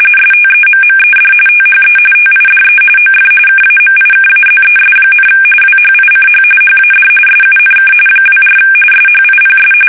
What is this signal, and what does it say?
RTTY, about 45.45 baud. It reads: STEMMED THE TIDE OF KAMIKAZES, WHICH HAD SUNK 24 SHIPS
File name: fsk850 / battleshipnj